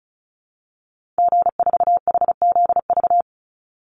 Morse code, heard as a keyed tone, about 34 wpm, 700 Hz. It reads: G4HZV
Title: Call Signs Repeated 3x - 35wpm